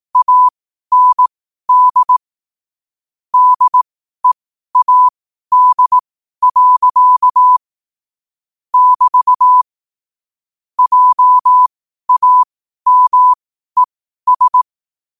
Morse code, heard as a keyed tone, about 18 wpm, spaced as for 12 wpm, 1000 Hz. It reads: AND DEAD. <BT> JAMES